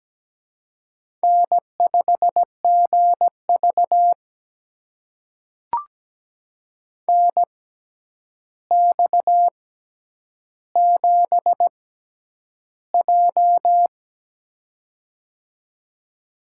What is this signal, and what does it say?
Morse code, heard as a keyed tone, about 17 wpm, spaced as for 9 wpm, 700 Hz. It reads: N5GV NX7J
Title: USA Call Signs - 6x Character Spacing - 17wpm